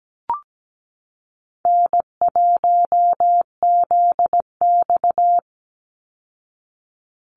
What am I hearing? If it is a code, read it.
Morse code, heard as a keyed tone, about 17 wpm, 700 Hz. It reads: N1ZX
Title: Easy Call Signs Repeated 3x - 17wpm